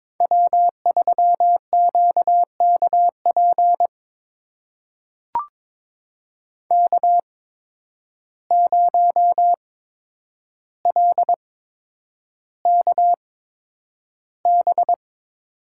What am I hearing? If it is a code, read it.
Morse code, heard as a keyed tone, about 21 wpm, spaced as for 9 wpm, 700 Hz. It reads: W3QKP K0LKB
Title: USA Call Signs - 5 Characters - 8x Character Spacing - 22wpm